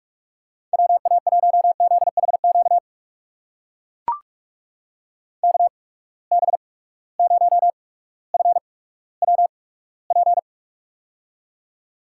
Morse code, heard as a keyed tone, about 43 wpm, spaced as for 19 wpm, 700 Hz. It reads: WA1ZHQ KB0FWP
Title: USA Call Signs - 6 Characters - 8x Character Spacing - 45wpm